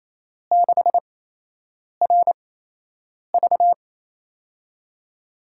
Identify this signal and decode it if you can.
Morse code, heard as a keyed tone, about 28 wpm, spaced as for 8 wpm, 700 Hz. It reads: 6RV
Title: USA Call Signs - 4 Characters - 8x Character Spacing - 28wpm